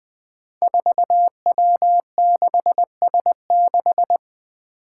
Morse code, heard as a keyed tone, about 20 wpm, 700 Hz. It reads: 4W6S6
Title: Five Challenging Letter-Number Combinations 20wpm